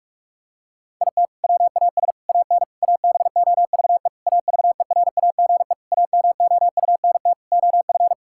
Morse code, heard as a keyed tone, about 46 wpm, 700 Hz. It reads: IT WAS AN ABOVE AVERAGE AMOUNT OF
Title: Sentences from Top 1000 Words - Review Part 6 - 45wpm